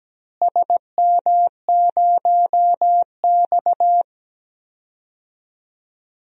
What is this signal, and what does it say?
Morse code, heard as a keyed tone, about 17 wpm, 700 Hz. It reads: SM0X
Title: DXCC-100 Call Signs - 4 Characters - Repeated 3x - 17wpm